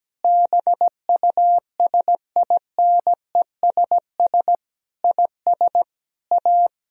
Morse code, heard as a keyed tone, about 17 wpm, 700 Hz. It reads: BUSINESS IS A
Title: Sentences from Top 1000 Words - Review Part 4 - 17wpm